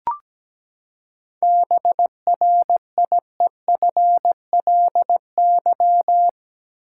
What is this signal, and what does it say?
Morse code, heard as a keyed tone, about 17 wpm, 700 Hz. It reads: BRIEFLY
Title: Seven Letter Words - 17wpm